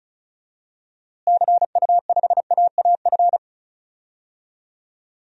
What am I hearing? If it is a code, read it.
Morse code, heard as a keyed tone, about 36 wpm, 700 Hz. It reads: CU5AAF
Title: DXCC-100 Call Signs - 6 Characters - Repeated 3x - 35wpm